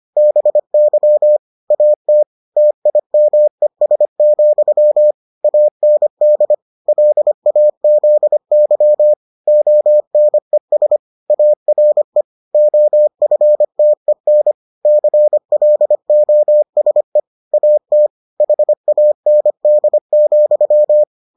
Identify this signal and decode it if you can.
Morse code, heard as a keyed tone, about 25 wpm, 600 Hz. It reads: BY AT TIMES, AND LAZY ONES ARE OFTEN CLOSE AT HAND,